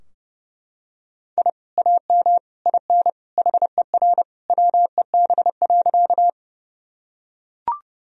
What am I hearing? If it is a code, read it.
Morse code, heard as a keyed tone, about 32 wpm, 700 Hz. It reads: I AM IN HER WEB.
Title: Sentences from Top 1000 Words - Review Part 4 - 30wpm